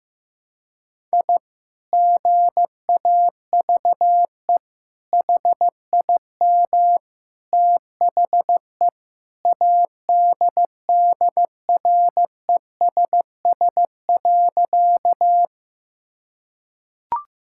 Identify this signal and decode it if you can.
Morse code, heard as a keyed tone, about 16 wpm, 700 Hz. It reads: I GAVE HIM THE ADDRESS.
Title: Sentences from Top 1000 Words - Review Part 6 - 15wpm